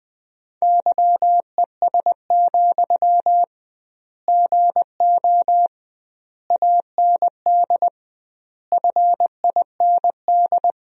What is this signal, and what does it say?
Morse code, heard as a keyed tone, about 20 wpm, 700 Hz. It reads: YES, GO AND FIND
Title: 2x Word-Spacing - Sentences from Top 500 Words - 20wpm